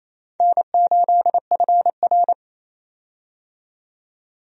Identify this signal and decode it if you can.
Morse code, heard as a keyed tone, about 28 wpm, 700 Hz. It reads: N8FR
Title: USA Call Signs - 4 Characters - 28wpm